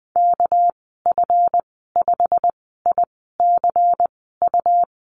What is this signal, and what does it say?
Morse code, heard as a keyed tone, about 19 wpm, spaced as for 14 wpm, 700 Hz. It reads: KF5ICU
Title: 238 Lesson 38g - Letter q - Call Signs 20wpm